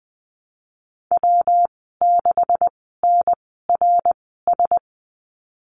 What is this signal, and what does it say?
Morse code, heard as a keyed tone, about 20 wpm, spaced as for 15 wpm, 700 Hz. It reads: W6NRS
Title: 173 Lesson 28g - Number 9 - Call Signs 20wpm